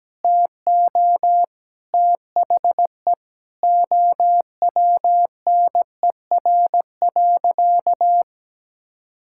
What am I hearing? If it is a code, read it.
Morse code, heard as a keyed tone, about 17 wpm, 700 Hz. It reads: TO THE OWNER.
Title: Sentences from Top 1600 Words 17wpm